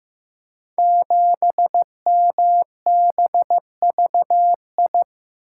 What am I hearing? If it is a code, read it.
Morse code, heard as a keyed tone, about 15 wpm, 700 Hz. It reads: 7MBVI
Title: Code Groups - 5 Letter:Numbers - 15wpm